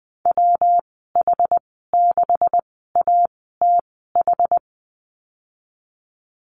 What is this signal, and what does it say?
Morse code, heard as a keyed tone, about 20 wpm, spaced as for 14 wpm, 700 Hz. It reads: WH6ATH
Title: 245 Lesson 39g - Letter z - Call Signs 20wpm